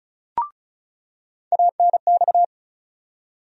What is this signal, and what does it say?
Morse code, heard as a keyed tone, about 36 wpm, 700 Hz. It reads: ANX
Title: USA Call Sign Postfixes - 3 Letters 35wpm